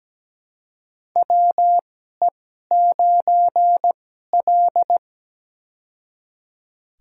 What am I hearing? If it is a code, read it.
Morse code, heard as a keyed tone, about 17 wpm, spaced as for 12 wpm, 700 Hz. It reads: WE9L
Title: USA Call Signs - 4 Characters - 2x Character Spacing - 17wpm